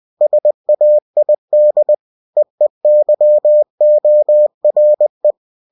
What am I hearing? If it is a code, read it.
Morse code, heard as a keyed tone, about 20 wpm, 600 Hz. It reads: SAID EEYORE
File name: cw-books / winnie-the-pooh / winnie-20wpm-page-0058